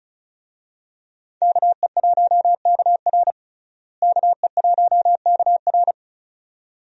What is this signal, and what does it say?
Morse code, heard as a keyed tone, about 34 wpm, 700 Hz. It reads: KE1KR KE1KR
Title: USA Call Signs - 5 Characters - Repeated 2x - 35wpm